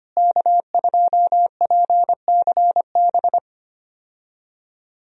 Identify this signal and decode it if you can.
Morse code, heard as a keyed tone, about 25 wpm, 700 Hz. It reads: K2PCB
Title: USA Call Signs - 5 Characters - Repeated 2x - 25wpm